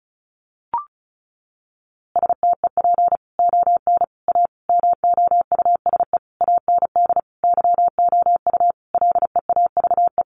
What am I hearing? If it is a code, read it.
Morse code, heard as a keyed tone, about 34 wpm, 700 Hz. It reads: STEP ON A MOUSE AND YOU LEAVE
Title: A Sound Of Thunder 35wpm